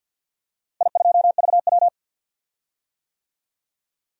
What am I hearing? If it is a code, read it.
Morse code, heard as a keyed tone, about 46 wpm, 700 Hz. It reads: I2VW
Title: DXCC-100 Call Signs - 4 Characters - Repeated 3x - 50wpm